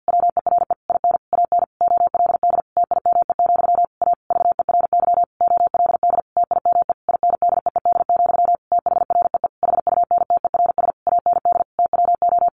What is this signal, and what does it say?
Morse code, heard as a keyed tone, about 50 wpm, 700 Hz. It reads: WERE IN AN OLD TIME, A VERY OLD TIME INDEED, THREE HUNTERS AND TWO
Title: A Sound Of Thunder 50wpm